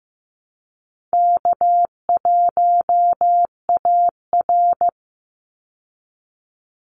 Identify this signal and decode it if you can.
Morse code, heard as a keyed tone, about 15 wpm, 700 Hz. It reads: K1AR
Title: Easy Call Signs Repeated 3x - 15wpm